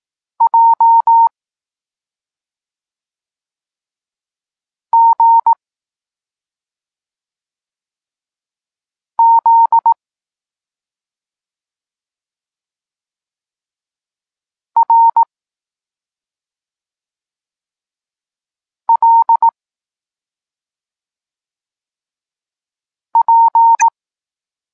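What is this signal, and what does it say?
Morse code, heard as a keyed tone, about 18 wpm, 920 Hz. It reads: J G Z R L P
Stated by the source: Morse code lesson for learning the letters K and X, from the SP9MOA Niepołomicki Klub Krótkofalowców course.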